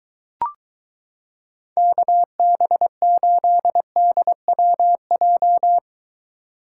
Morse code, heard as a keyed tone, about 22 wpm, 700 Hz. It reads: KB8DWJ